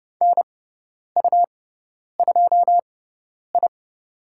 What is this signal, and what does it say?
Morse code, heard as a keyed tone, about 30 wpm, spaced as for 10 wpm, 700 Hz. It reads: NU2I
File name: Lesson 18g - Letter m - Call Signs 30wpm